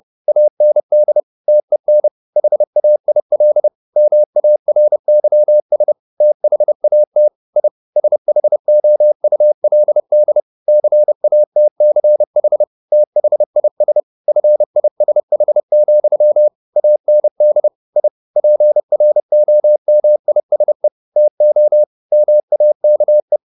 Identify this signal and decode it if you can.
Morse code, heard as a keyed tone, about 30 wpm, 600 Hz. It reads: AND TEN HAIL MARYS THAT I SHOULD CATCH THIS FISH, AND I PROMISE TO MAKE
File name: the old man and the sea / the old man and the sea 1 30wpm